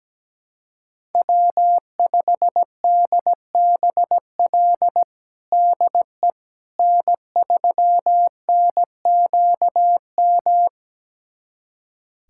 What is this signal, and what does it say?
Morse code, heard as a keyed tone, about 17 wpm, 700 Hz. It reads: W5DBL DE N3NQM
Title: USA Callsign DE Callsign - 5 Characters - 17wpm